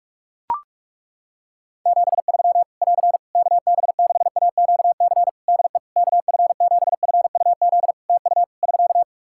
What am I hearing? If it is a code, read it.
Morse code, heard as a keyed tone, about 42 wpm, 700 Hz. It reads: 73 <AR> KB6AQC DE KF7FUZ TU <SK>
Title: Practice QSO - Exchange 4 - Ending - 45wpm